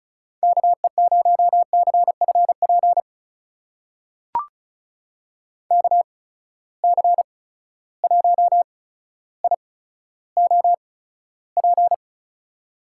Morse code, heard as a keyed tone, about 36 wpm, spaced as for 15 wpm, 700 Hz. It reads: KE0CFP KC1IOP
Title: USA Call Signs - 6 Characters - 8x Character Spacing - 35wpm